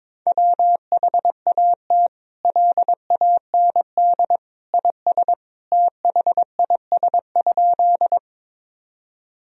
Morse code, heard as a keyed tone, about 22 wpm, 700 Hz. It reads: WHAT LAND IS THIS?
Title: Sentences from Top 1000 Words - Review Part 1 - 22wpm